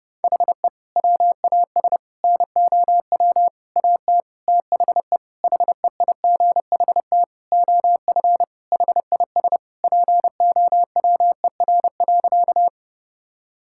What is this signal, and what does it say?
Morse code, heard as a keyed tone, about 30 wpm, 700 Hz. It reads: HE WAS NOW AT THE HEIGHT OF HIS POWER.
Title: Sentences from Top 1600 Words 30wpm